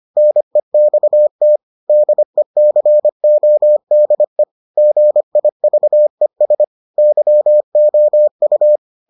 Morse code, heard as a keyed tone, about 25 wpm, 600 Hz. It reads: NEXT DECODE GIVES YOU